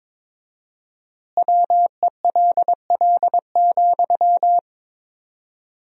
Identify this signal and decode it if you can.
Morse code, heard as a keyed tone, about 22 wpm, 700 Hz. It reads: WELL,
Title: 4x Word-Spacing - Sentences from Top 500 Words - 22wpm